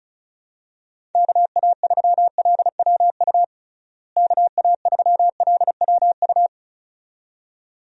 Morse code, heard as a keyed tone, about 34 wpm, 700 Hz. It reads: KA3LWU KA3LWU
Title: USA Call Signs - 6 Characters - Repeated 2x - 35wpm